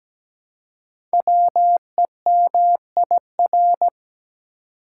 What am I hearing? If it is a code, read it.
Morse code, heard as a keyed tone, about 17 wpm, 700 Hz. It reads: WEMIR
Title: Code Groups - 5 Letters - 17wpm